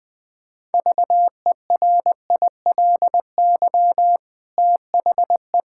Morse code, heard as a keyed tone, about 20 wpm, 700 Hz. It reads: VERILY THE
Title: Pangram Challenge v2 - 20wpm